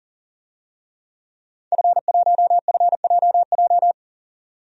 Morse code, heard as a keyed tone, about 40 wpm, 700 Hz. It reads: F1FJJ
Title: DXCC-100 Call Signs - 5 Characters - 40wpm